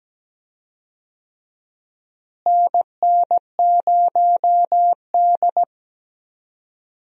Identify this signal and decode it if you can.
Morse code, heard as a keyed tone, about 17 wpm, 700 Hz. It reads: NN0D
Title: USA Call Signs - 4 Characters - Repeated 2x - 17wpm